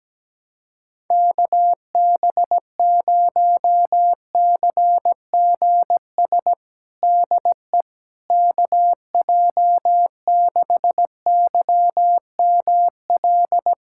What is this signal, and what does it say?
Morse code, heard as a keyed tone, about 17 wpm, 700 Hz. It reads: KB0CGS DE KJ6YML
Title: USA Callsign DE Callsign - 6 Characters - 17wpm